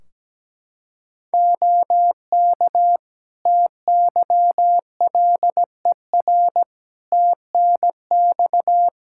Morse code, heard as a keyed tone, about 17 wpm, 700 Hz. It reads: OK TYLER TNX
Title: Practice QSO - Exchange 4 - Ending - 17wpm